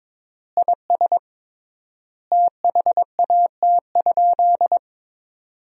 Morse code, heard as a keyed tone, about 22 wpm, 700 Hz. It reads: IS THAT?
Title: 3x Word-Spacing - Sentences from Top 100 Words - 22wpm